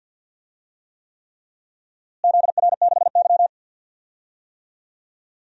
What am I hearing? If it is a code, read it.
Morse code, heard as a keyed tone, about 50 wpm, 700 Hz. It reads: ZR6Y
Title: DXCC-100 Call Signs - 4 Characters - Repeated 3x - 50wpm